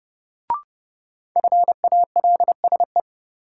Morse code, heard as a keyed tone, about 30 wpm, 700 Hz. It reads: FALSE
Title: Sentences from Top 1600 Words - New Word Drill Alt - 30wpm